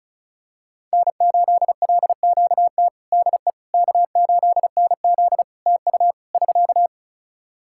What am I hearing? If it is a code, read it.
Morse code, heard as a keyed tone, about 36 wpm, 700 Hz. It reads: N8LQT DE K8NZ TU <SK>
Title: Practice QSO - Exchange 4 - Ending - 35wpm